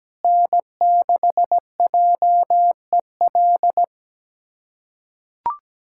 Morse code, heard as a keyed tone, about 17 wpm, 700 Hz. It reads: N6JEL
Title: USA Callsign DE Callsign - 5 Characters - 17wpm